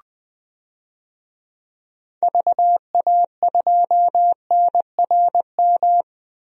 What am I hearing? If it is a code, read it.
Morse code, heard as a keyed tone, about 20 wpm, 700 Hz. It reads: VA2NRM
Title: DXCC-100 Call Signs - 6 Characters - 20wpm